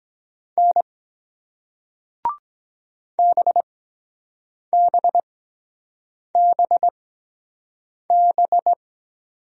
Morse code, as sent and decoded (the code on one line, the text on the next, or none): -. / -... -... -... -...
N BBBB